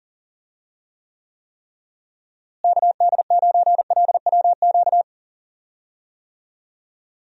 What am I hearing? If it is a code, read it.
Morse code, heard as a keyed tone, about 40 wpm, 700 Hz. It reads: KD9LWQ
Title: USA Call Signs - 6 Characters - Repeated 3x - 40wpm